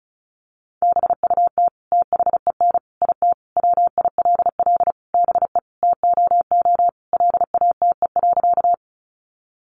Morse code, heard as a keyed tone, about 34 wpm, 700 Hz. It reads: BUT THEN IT WILL BE TOO LATE.
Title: Sentences from Top 300 Words 35wpm